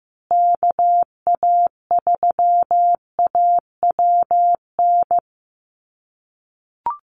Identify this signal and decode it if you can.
Morse code, heard as a keyed tone, about 15 wpm, 700 Hz. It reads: KA3AWN
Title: USA Call Signs Repeated 2x 15wpm